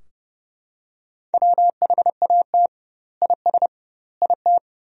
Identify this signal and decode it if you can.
Morse code, heard as a keyed tone, about 30 wpm, 700 Hz. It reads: WHAT IS IT
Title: 2x Word-Spacing - Sentences from Top 100 Words - 30wpm